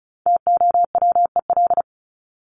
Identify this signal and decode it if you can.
Morse code, heard as a keyed tone, about 34 wpm, 700 Hz. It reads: TOWEL
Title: Top 1501-1750 Words 35wpm